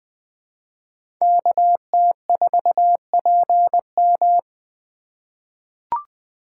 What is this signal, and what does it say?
Morse code, heard as a keyed tone, about 20 wpm, 700 Hz. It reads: KT4PM
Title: USA Call Signs - 5 Characters - 6x Character Spacing - 20wpm